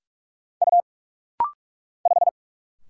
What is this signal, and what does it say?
Morse code, heard as a keyed tone, about 46 wpm, 700 Hz. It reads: U 5
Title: Single Letter-Number Alt - Rapid-Fire 45wpm